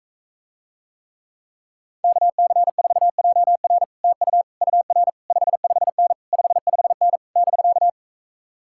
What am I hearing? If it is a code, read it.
Morse code, heard as a keyed tone, about 45 wpm, 700 Hz. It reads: KC4JR TU UR 55N 55N <BK>